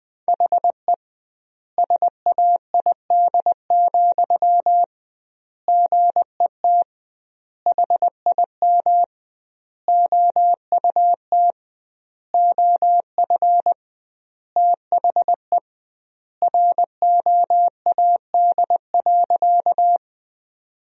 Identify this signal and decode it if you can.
Morse code, heard as a keyed tone, about 20 wpm, 700 Hz. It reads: HE SAID, GET HIM OUT OF THE ROAD.
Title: 2x Word-Spacing - Sentences from Top 400 Words - 20wpm